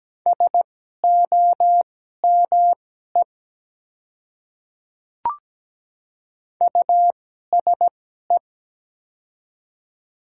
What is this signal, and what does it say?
Morse code, heard as a keyed tone, about 17 wpm, spaced as for 12 wpm, 690 Hz. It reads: SOME USE
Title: 2x Character Spacing - Top 100 Words - 17wpm